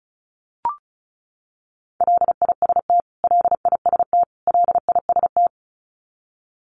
Morse code, heard as a keyed tone, about 34 wpm, 700 Hz. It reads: LIST LIST LIST
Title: Top 500 Words Repeated 35wpm